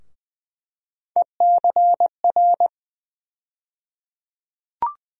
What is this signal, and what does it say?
Morse code, heard as a keyed tone, about 20 wpm, 700 Hz. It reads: ECR
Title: USA Call Sign Postfixes - 3 Letters 20wpm